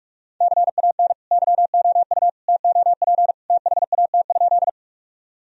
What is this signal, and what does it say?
Morse code, heard as a keyed tone, about 43 wpm, 700 Hz. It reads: CAN YOU TOP THAT?
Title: Sentences from Top 1000 Words - Review Part 2 - 45wpm